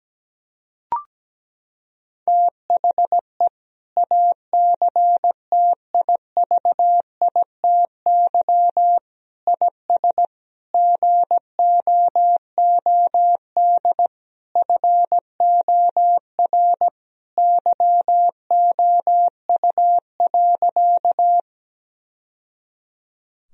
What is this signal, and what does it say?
Morse code, heard as a keyed tone, about 17 wpm, 700 Hz. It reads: THE ACTIVITY IS GOOD FOR YOU.
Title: Sentences from Top 1000 Words - Review Part 4 - 17wpm